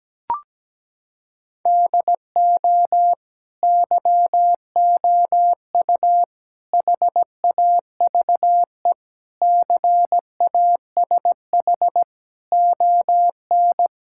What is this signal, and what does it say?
Morse code, heard as a keyed tone, about 17 wpm, 700 Hz. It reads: DO YOU HAVE CASH ON
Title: Sentences from Top 1000 Words - Review Part 5 - 17wpm